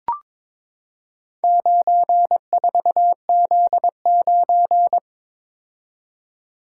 Morse code, heard as a keyed tone, about 22 wpm, 700 Hz. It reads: 94Z9
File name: practice / 22wpm / Four Letter-Number Combinations - 22wpm